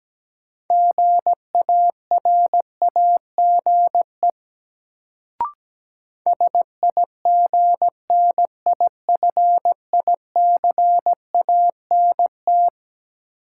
Morse code, heard as a keyed tone, about 17 wpm, 700 Hz. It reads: GARAGE SIGNIFICANT
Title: Sentences from Top 1600 Words - New Word Drill Alt - 17wpm